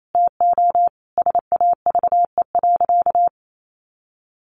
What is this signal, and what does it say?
Morse code, heard as a keyed tone, about 28 wpm, 700 Hz. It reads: TO SAVE.
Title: Sentences from Top 1100 Words 28wpm